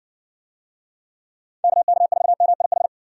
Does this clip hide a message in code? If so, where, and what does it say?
Yes: Morse code from 1.6 to 2.9 s, heard as a keyed tone, about 60 wpm, 700 Hz. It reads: KK4MIH